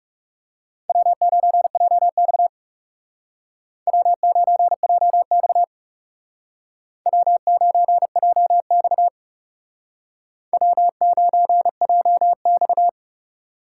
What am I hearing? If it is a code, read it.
Morse code, heard as a keyed tone, about 37 wpm, 700 Hz. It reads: W9JX W9JX W9JX W9JX
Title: Speed Racing - USA 1x2 Call Signs 30wpm